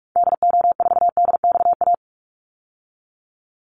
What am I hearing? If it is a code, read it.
Morse code, heard as a keyed tone, about 44 wpm, 700 Hz. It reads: DO4DXA
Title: Call Signs Repeated 3x - 45wpm